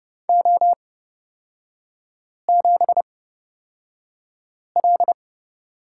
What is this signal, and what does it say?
Morse code, heard as a keyed tone, about 30 wpm, spaced as for 5 wpm, 700 Hz. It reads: O7L
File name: Lesson 37e - Letter x - Three Characters 30wpm